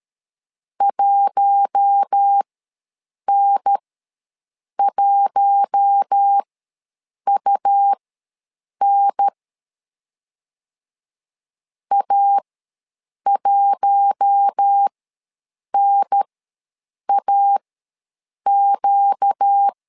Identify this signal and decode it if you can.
Morse code, heard as a keyed tone, about 13 wpm, spaced as for 7 wpm, 800 Hz. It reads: 1N1UN A1NAQ
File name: LEZIONI / 4 Q1 ws 12wpm 2129